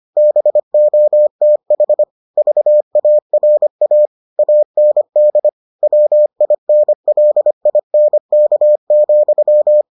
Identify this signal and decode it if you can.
Morse code, heard as a keyed tone, about 25 wpm, 600 Hz. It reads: BOTH VARA AND WINLINK,